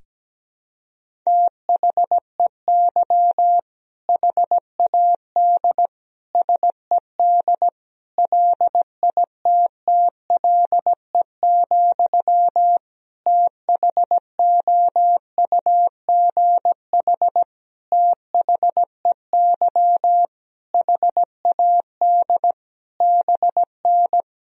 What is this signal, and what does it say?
Morse code, heard as a keyed tone, about 17 wpm, 700 Hz. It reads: THEY HAD SED LITTLE, THOUGH THEY HAD BN
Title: Sentences from Top 400 Words and Abbreviations 17wpm